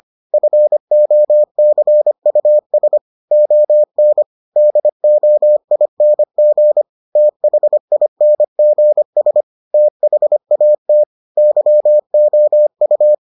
Morse code, heard as a keyed tone, about 25 wpm, 600 Hz. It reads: FOCUS ON DOING THINGS THAT YOU